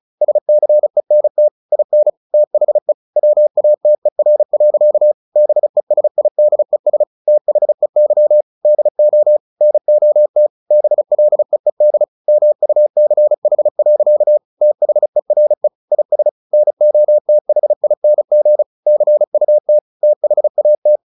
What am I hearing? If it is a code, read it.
Morse code, heard as a keyed tone, about 34 wpm, 600 Hz. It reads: SCENT IN THE WATER. BESIDES THEY DO NOT BLEED MUCH. THERE IS NOTHING CUT THAT
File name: the old man and the sea 2 35wpm